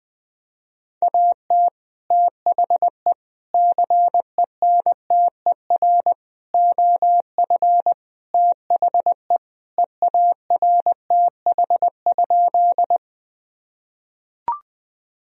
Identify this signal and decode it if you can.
Morse code, heard as a keyed tone, about 20 wpm, 700 Hz. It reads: AT THE CENTER OF THE EARTH?
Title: Sentences from Top 1000 Words - Review Part 2 - 20wpm